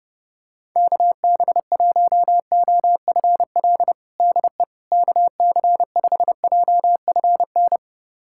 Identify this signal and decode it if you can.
Morse code, heard as a keyed tone, about 30 wpm, 700 Hz. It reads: KB1OFL DE KC5JFN
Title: USA Callsign DE Callsign - 6 Characters - 30wpm